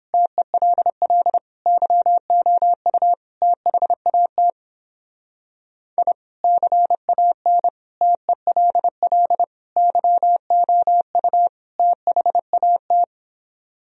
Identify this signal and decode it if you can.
Morse code, heard as a keyed tone, about 31 wpm, 700 Hz. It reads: TELL YOU THAT I CAN TELL YOU THAT